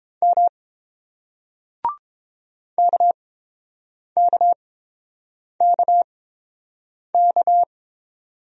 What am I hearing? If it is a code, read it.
Morse code, heard as a keyed tone, about 30 wpm, spaced as for 7 wpm, 690 Hz. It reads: M KKKK